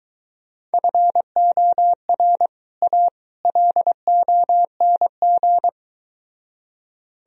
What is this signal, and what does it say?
Morse code, heard as a keyed tone, about 23 wpm, 700 Hz. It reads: FOR A LONG